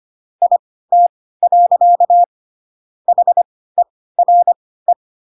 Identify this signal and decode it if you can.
Morse code, heard as a keyed tone, about 25 wpm, spaced as for 16 wpm, 700 Hz. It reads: IT. HERE
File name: Pinocchio / Pinocchio ch1 10 25 ik1hge